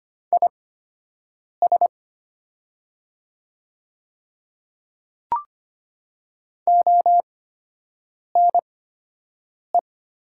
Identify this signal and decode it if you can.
Morse code, heard as a keyed tone, about 25 wpm, spaced as for 7 wpm, 700 Hz. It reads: IS ONE
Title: 8x Character Spacing - Top 200 Words - 25wpm